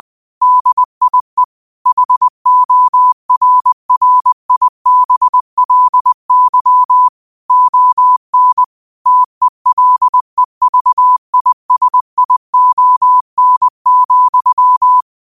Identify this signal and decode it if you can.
Morse code, heard as a keyed tone, about 20 wpm, 1000 Hz. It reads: DIE HORRIBLY ON TELEVISION,
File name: qotdfiles / 20wpm